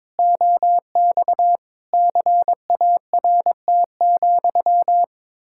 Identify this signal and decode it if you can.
Morse code, heard as a keyed tone, about 22 wpm, 700 Hz. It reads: OX CART,
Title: Pangram Challenge v2 - 22wpm